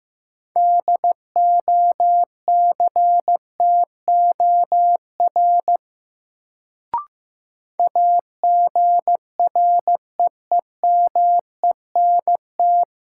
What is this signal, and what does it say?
Morse code, heard as a keyed tone, about 15 wpm, 700 Hz. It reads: DOCTOR AGREEMENT
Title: Sentences from Top 1500 Words - New Word Drill Alt - 15wpm